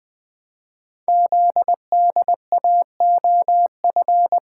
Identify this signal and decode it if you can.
Morse code, heard as a keyed tone, about 20 wpm, 700 Hz. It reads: ZDAOF
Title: Code Groups - 5 Letter:Numbers - 20wpm